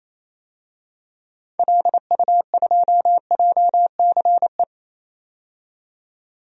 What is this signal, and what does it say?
Morse code, heard as a keyed tone, about 27 wpm, 700 Hz. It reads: LU2JCE
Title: DXCC-100 Call Signs - 6 Characters - 28wpm